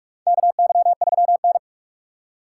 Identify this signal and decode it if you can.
Morse code, heard as a keyed tone, about 44 wpm, 700 Hz. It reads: KY3N
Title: USA Call Signs - 6x Character Spacing - 45wpm